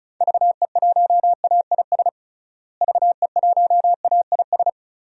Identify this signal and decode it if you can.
Morse code, heard as a keyed tone, about 36 wpm, 700 Hz. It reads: VE1AIS VE1AIS
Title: DXCC-100 Call Signs - 6 Characters - Repeated 2x - 35wpm